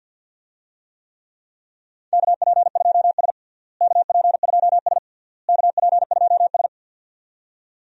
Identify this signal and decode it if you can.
Morse code, heard as a keyed tone, about 53 wpm, 700 Hz. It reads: KP2S KP2S KP2S
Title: DXCC-100 Call Signs - 4 Characters - Repeated 3x - 50wpm